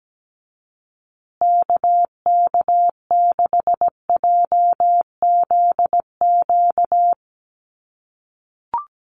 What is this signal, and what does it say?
Morse code, heard as a keyed tone, about 17 wpm, 700 Hz. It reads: KK6JZQ
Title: USA Call Signs Repeated 2x 17wpm